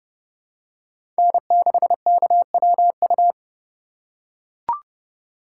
Morse code, heard as a keyed tone, about 30 wpm, 700 Hz. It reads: N6KWU
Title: USA Call Signs - 5 Characters - 4x Character Spacing - 30wpm